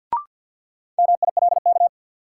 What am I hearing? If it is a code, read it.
Morse code, heard as a keyed tone, about 49 wpm, 700 Hz. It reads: MILK
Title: Sentences from Top 1500 Words - New Word Drill Alt - 50wpm